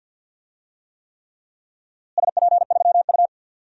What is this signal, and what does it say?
Morse code, heard as a keyed tone, about 46 wpm, 700 Hz. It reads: SP3U
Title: DXCC-100 Call Signs - 4 Characters - Repeated 3x - 50wpm